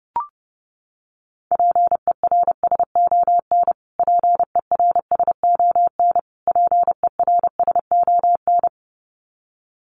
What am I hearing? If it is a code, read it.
Morse code, heard as a keyed tone, about 30 wpm, 700 Hz. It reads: PERSON PERSON PERSON
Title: Top 500 Words Repeated 30wpm